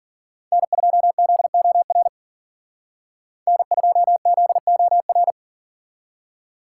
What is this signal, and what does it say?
Morse code, heard as a keyed tone, about 48 wpm, 700 Hz. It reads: N2ZOR N2ZOR
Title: Speed Racing - USA Call Signs 40wpm